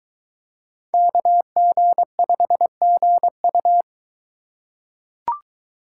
Morse code, heard as a keyed tone, about 22 wpm, 700 Hz. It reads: KG5GU